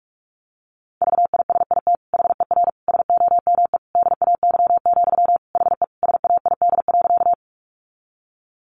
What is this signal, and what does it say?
Morse code, heard as a keyed tone, about 44 wpm, 700 Hz. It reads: VISIT HER SOME DAY, HE SAID.
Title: Sentences from Top 900 Words 45wpm